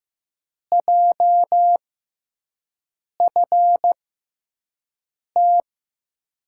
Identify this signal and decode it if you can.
Morse code, heard as a keyed tone, about 15 wpm, spaced as for 5 wpm, 700 Hz. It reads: JFT
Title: USA Call Signs - 5 Characters - 6x Character Spacing - 15wpm